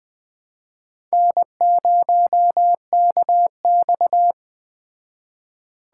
Morse code, heard as a keyed tone, about 20 wpm, 700 Hz. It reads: N0KX
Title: Speed Racing - USA 1x2 Call Signs 17wpm